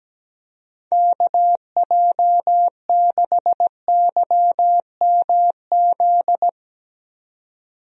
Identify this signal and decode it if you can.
Morse code, heard as a keyed tone, about 17 wpm, 700 Hz. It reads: KJ6YMZ
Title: USA Call Signs - 6 Characters - 17wpm